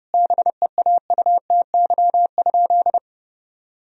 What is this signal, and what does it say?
Morse code, heard as a keyed tone, about 32 wpm, 700 Hz. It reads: BEAUTY?
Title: Sentences from Top 1000 Words - Review Part 2 - 30wpm